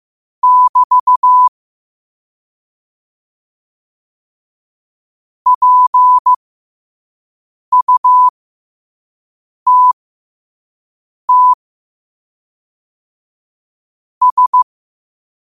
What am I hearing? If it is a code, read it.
Morse code, heard as a keyed tone, about 15 wpm, spaced as for 5 wpm, 1000 Hz. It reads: <BT> PUTT S
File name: qotdfiles / 5wpm